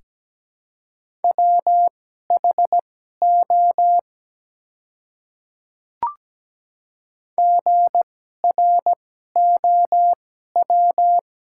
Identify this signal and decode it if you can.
Morse code, heard as a keyed tone, about 17 wpm, spaced as for 12 wpm, 700 Hz. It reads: WHO GROW
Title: 2x Character Spacing - Top 300 Words - 17wpm